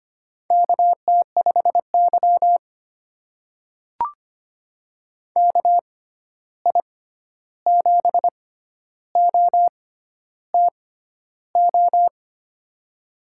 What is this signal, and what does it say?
Morse code, heard as a keyed tone, about 25 wpm, spaced as for 11 wpm, 700 Hz. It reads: KT5Y KI7OTO
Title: USA Call Signs - 6x Character Spacing - 25wpm